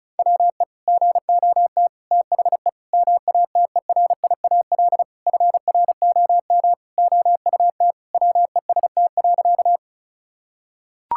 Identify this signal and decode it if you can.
Morse code, heard as a keyed tone, about 34 wpm, 700 Hz. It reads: WE GOT THE MATERIAL FROM OUT WEST.
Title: Sentences from Top 1000 Words - Review Part 5 - 35wpm